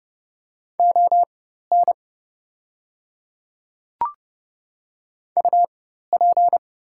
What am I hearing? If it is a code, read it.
Morse code, heard as a keyed tone, about 30 wpm, spaced as for 14 wpm, 700 Hz. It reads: ON UP
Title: 4x Character Spacing - Top 100 Words - 30wpm